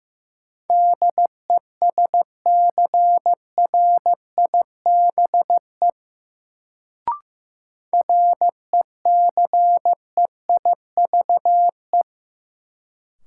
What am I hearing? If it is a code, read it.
Morse code, heard as a keyed tone, about 15 wpm, 700 Hz. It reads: DESCRIBE RECEIVE
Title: Sentences from Top 1000 Words - New Word Drill - 15wpm